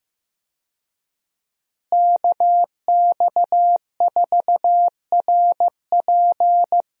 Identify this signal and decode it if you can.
Morse code, heard as a keyed tone, about 15 wpm, 700 Hz. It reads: KX4RP
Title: USA Call Signs - 5 Characters - Repeated 2x - 15wpm